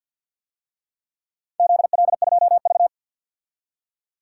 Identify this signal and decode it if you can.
Morse code, heard as a keyed tone, about 51 wpm, 700 Hz. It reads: ZL2V
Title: DXCC-100 Call Signs - 4 Characters - 50wpm